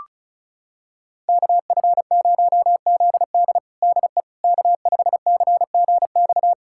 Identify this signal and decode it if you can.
Morse code, heard as a keyed tone, about 36 wpm, 700 Hz. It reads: KF0ZD DE K5CGX
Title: USA Callsign DE Callsign - 5 Characters - 35wpm